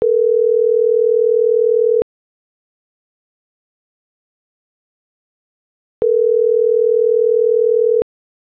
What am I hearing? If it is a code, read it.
Ringback tone (North America), 440 + 480 Hz.